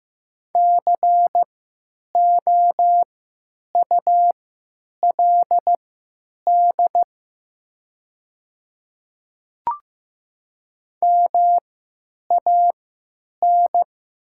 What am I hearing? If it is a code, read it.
Morse code, heard as a keyed tone, about 15 wpm, spaced as for 8 wpm, 700 Hz. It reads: COULD MAN
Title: 3x Character Spacing - Top 200 Words - 15wpm